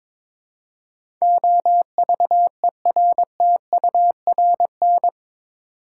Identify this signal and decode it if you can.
Morse code, heard as a keyed tone, about 21 wpm, 700 Hz. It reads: OVERTURN
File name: practice / 22wpm / Eight Letter Words - 22wpm